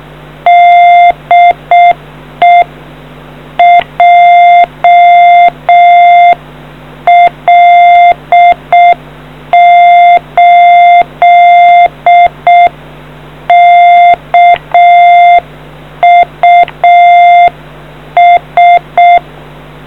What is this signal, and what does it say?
Morse code, heard as a keyed tone, about 6 wpm, 700 Hz. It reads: DE JL8KUS